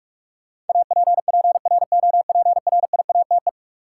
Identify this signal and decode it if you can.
Morse code, heard as a keyed tone, about 46 wpm, 700 Hz. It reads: APPROPRIATE
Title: Sentences from Top 900 Words - New Word Drill - 45wpm